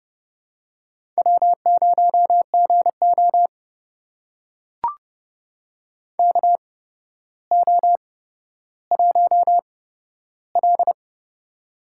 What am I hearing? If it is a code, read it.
Morse code, heard as a keyed tone, about 32 wpm, spaced as for 13 wpm, 700 Hz. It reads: W0GO KO1L
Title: USA Call Signs - 4 Characters - 8x Character Spacing - 30wpm